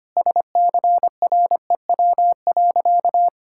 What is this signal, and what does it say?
Morse code, heard as a keyed tone, about 25 wpm, 700 Hz. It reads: SCREW.
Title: Sentences from Top 1600 Words 25wpm